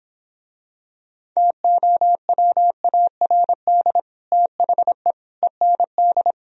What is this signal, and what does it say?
Morse code, heard as a keyed tone, about 26 wpm, 700 Hz. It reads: TOWARD THE END
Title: Speed-Racing - Sets of 5 Words 22wpm